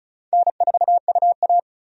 Morse code, heard as a keyed tone, about 36 wpm, 700 Hz. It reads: N4UA
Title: USA Call Signs - 4 Characters - 35wpm